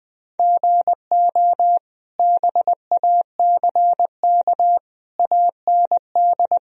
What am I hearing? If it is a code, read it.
Morse code, heard as a keyed tone, about 20 wpm, 700 Hz. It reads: GO BACK AND
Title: Sentences from Top 1000 Words - Review Part 1 - 20wpm